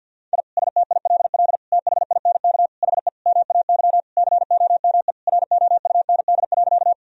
Morse code, heard as a keyed tone, about 51 wpm, 700 Hz. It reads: I STILL THINK HE MAY COME ROUND.
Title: Sentences from Top 1000 Words - Review Part 1 - 50wpm